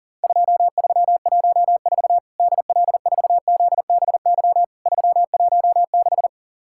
Morse code, heard as a keyed tone, about 43 wpm, 700 Hz. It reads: 2314 DL4ZBY 316
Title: Worked All Europe DX Contest - QTC Exchange - 40wpm